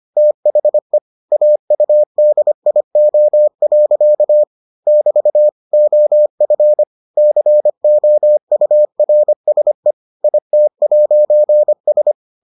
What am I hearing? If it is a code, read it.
Morse code, heard as a keyed tone, about 25 wpm, 600 Hz. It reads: THE AUDIO. <BT> OF COURSE IT'S